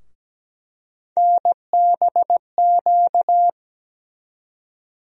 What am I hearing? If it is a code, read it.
Morse code, heard as a keyed tone, about 17 wpm, 700 Hz. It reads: NBQ
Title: USA Call Sign Postfixes - 3 Letters 17wpm